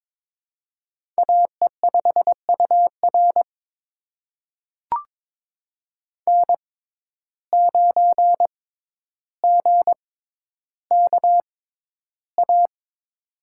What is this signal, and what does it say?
Morse code, heard as a keyed tone, about 22 wpm, spaced as for 11 wpm, 700 Hz. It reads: AE5UR N9GKA
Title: USA Call Signs - 5 Characters - 6x Character Spacing - 22wpm